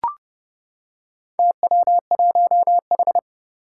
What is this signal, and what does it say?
Morse code, heard as a keyed tone, about 30 wpm, 700 Hz. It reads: TW1H
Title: Four Challenging Letter-Number Combinations 30wpm